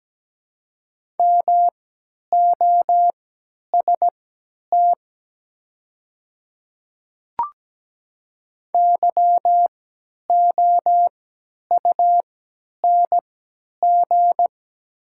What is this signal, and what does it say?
Morse code, heard as a keyed tone, about 17 wpm, spaced as for 10 wpm, 700 Hz. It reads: MOST YOUNG
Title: 3x Character Spacing - Top 500 Words - 17wpm